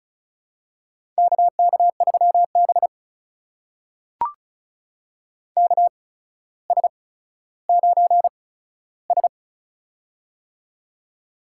Morse code, heard as a keyed tone, about 36 wpm, spaced as for 15 wpm, 700 Hz. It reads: KK3B KS9S
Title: USA Call Signs - 4 Characters - 8x Character Spacing - 35wpm